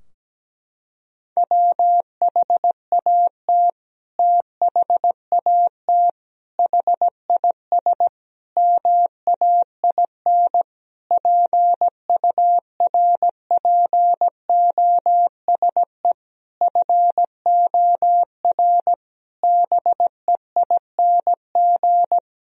Morse code, heard as a keyed tone, about 17 wpm, 700 Hz. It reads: WHAT THAT HIS MAIN PURPOSE FOR BEING
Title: Sentences from Top 1000 Words - Review Part 2 - 17wpm